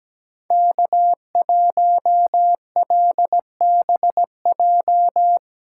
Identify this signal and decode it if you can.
Morse code, heard as a keyed tone, about 17 wpm, 700 Hz. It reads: K1LBJ
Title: USA Callsign DE Callsign - 5 Characters - 17wpm